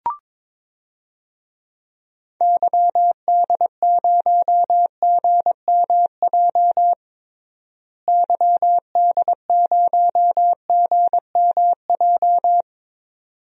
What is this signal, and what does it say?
Morse code, heard as a keyed tone, about 22 wpm, 700 Hz. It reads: YD0GMJ YD0GMJ
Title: DXCC-100 Call Signs - 6 Characters - Repeated 2x - 22wpm